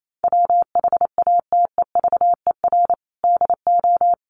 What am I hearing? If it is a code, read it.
Morse code, heard as a keyed tone, about 28 wpm, 700 Hz. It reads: WHATEVER DO
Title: Sentences from Top 1100 Words 28wpm